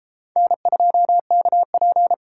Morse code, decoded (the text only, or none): N2KP